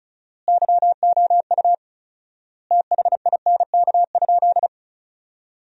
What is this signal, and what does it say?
Morse code, heard as a keyed tone, about 36 wpm, 700 Hz. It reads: YOU THINK?
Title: 4x Word-Spacing - Sentences from Top 200 Words - 35wpm